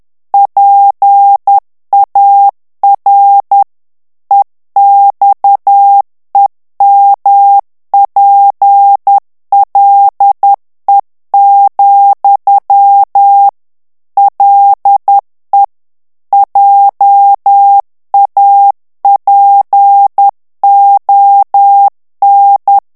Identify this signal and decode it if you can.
Morse code, heard as a keyed tone, about 10 wpm, 800 Hz. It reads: PAR EXEMPLE, LE JAPON
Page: Français - texte en clair 1 - vitesse : 10 mots minute